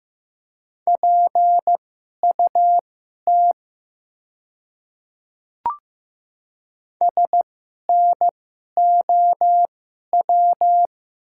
Morse code, heard as a keyed tone, about 15 wpm, spaced as for 11 wpm, 700 Hz. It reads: PUT SNOW
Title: 2x Character Spacing - Top 500 Words - 15wpm